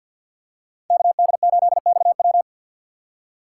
Morse code, heard as a keyed tone, about 53 wpm, 700 Hz. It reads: KD8XW
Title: USA Call Signs - 5 Characters - 6x Character Spacing - 50wpm